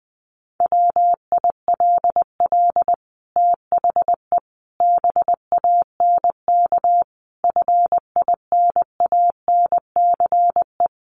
Morse code, heard as a keyed tone, about 20 wpm, 700 Hz. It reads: WILL THE BANK FINANCE
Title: Sentences from Top 1500 Words 20wpm